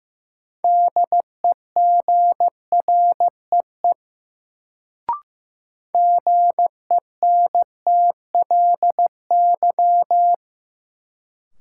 Morse code, heard as a keyed tone, about 15 wpm, 700 Hz. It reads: DEGREE GENTLY
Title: Sentences from Top 1500 Words - New Word Drill Alt - 15wpm